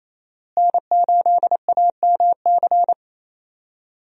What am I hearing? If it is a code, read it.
Morse code, heard as a keyed tone, about 28 wpm, 700 Hz. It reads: N8AMC
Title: USA Call Signs - 5 Characters - Repeated 3x - 28wpm